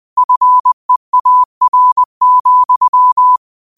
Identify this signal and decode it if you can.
Morse code, heard as a keyed tone, about 20 wpm, 1000 Hz. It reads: FEAR,
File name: qotdfiles / 20wpm